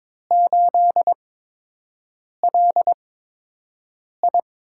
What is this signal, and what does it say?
Morse code, heard as a keyed tone, about 22 wpm, spaced as for 6 wpm, 700 Hz. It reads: 8LI
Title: USA Call Signs - 8x Character Spacing - 22wpm